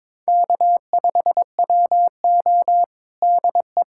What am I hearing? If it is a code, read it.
Morse code, heard as a keyed tone, about 22 wpm, 700 Hz. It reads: K5WO DE
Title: USA Callsign DE Callsign - 4 Characters - 22wpm